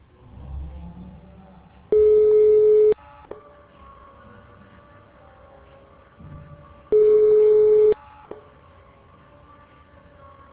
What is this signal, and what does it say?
Ringback tone (Europe, 425 Hz).